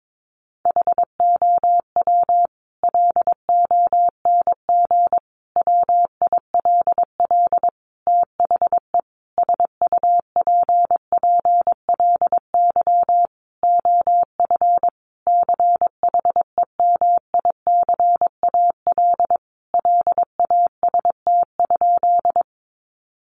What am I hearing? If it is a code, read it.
Morse code, heard as a keyed tone, about 22 wpm, 700 Hz. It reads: HOW LONG WILL THE SUPPLY OF CHEMICAL LAST?
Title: Sentences from Top 1200 Words 22wpm